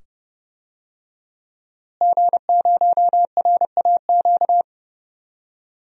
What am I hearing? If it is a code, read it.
Morse code, heard as a keyed tone, about 30 wpm, 700 Hz. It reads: G0RAQ
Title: DXCC-100 Call Signs - 5 Characters - Repeated 2x - 30wpm